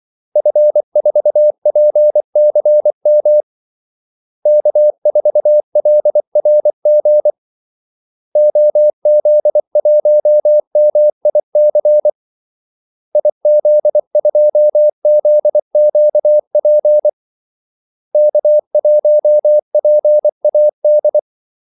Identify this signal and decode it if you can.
Morse code, heard as a keyed tone, about 24 wpm, 600 Hz. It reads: F4PCM K4LRG OZ1MIC IZ2ZQP K1PAD
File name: Callsigns-24wpm